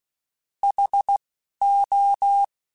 Morse code, heard as a keyed tone, about 16 wpm, spaced as for 11 wpm, 800 Hz. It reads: HO
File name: Alphabetical / Alphabetical mp3s / Hs @10